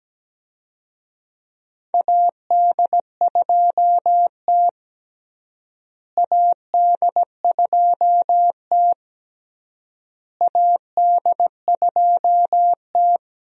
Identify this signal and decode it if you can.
Morse code, heard as a keyed tone, about 17 wpm, 700 Hz. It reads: AD2T AD2T AD2T
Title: USA Call Signs - 4 Characters - Repeated 3x - 17wpm